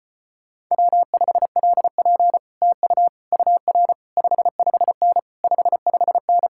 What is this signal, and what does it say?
Morse code, heard as a keyed tone, about 33 wpm, 700 Hz. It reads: W5LP TU UR 55N 55N